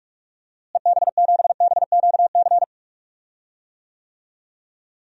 Morse code, heard as a keyed tone, about 41 wpm, 700 Hz. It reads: EB7BQC
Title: DXCC-100 Call Signs - 6 Characters - Repeated 2x - 45wpm